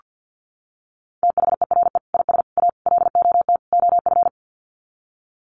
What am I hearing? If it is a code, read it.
Morse code, heard as a keyed tone, about 48 wpm, 700 Hz. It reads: THERE IS A LOT OF
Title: Sets of 5 Words 50wpm